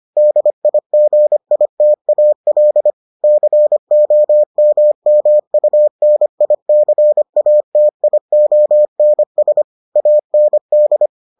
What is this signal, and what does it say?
Morse code, heard as a keyed tone, about 25 wpm, 600 Hz. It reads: DIGITAL COMMUNICATIONS AND